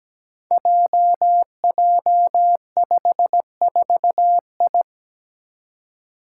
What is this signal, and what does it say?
Morse code, heard as a keyed tone, about 17 wpm, 700 Hz. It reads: JJ54I
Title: Five Challenging Letter-Number Combinations 17wpm